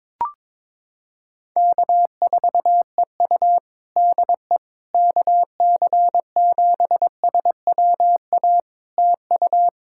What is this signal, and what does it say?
Morse code, heard as a keyed tone, about 22 wpm, 700 Hz. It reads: K4EU DE KC7SWA TU
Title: Practice QSO - Exchange 4 - Ending - 22wpm